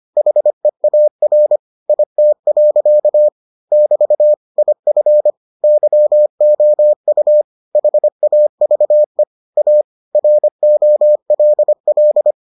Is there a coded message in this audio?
Yes: Morse code from 0.2 to 12.3 s, heard as a keyed tone, about 25 wpm, 600 Hz. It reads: HEAR IT. <BT> IF YOU HAVE A ROLL